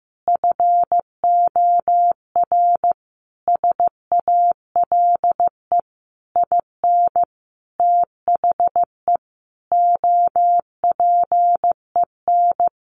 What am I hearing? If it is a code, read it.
Morse code, heard as a keyed tone, about 15 wpm, 700 Hz. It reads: FOR SALE IN THE OPEN
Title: Sentences from Top 1400 Words 15wpm